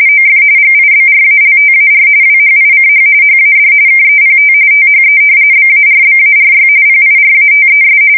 RTTY, about 45.45 baud. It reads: OF GIBRALTAR WHICH HAD KEPT THE BARBARY PIRATES
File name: fsk170 / usnavy1